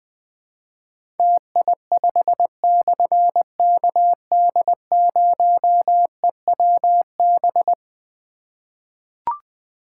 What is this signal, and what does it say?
Morse code, heard as a keyed tone, about 20 wpm, 700 Hz. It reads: TI5/KD0EWB
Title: Difficult Call Signs 20wpm